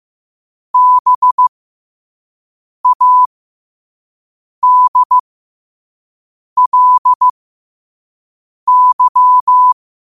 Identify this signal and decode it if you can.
Morse code, heard as a keyed tone, about 15 wpm, spaced as for 5 wpm, 1000 Hz. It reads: BADLY
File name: qotdfiles / 5wpm